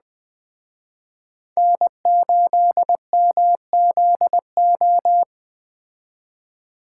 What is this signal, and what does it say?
Morse code, heard as a keyed tone, about 20 wpm, 700 Hz. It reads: N8MZO
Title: Speed Racing - USA 1x3 Call Signs 17wpm